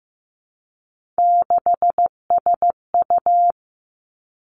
Morse code, heard as a keyed tone, about 15 wpm, 700 Hz. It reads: 6SU
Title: Three Challenging Letter-Number Combinations 15wpm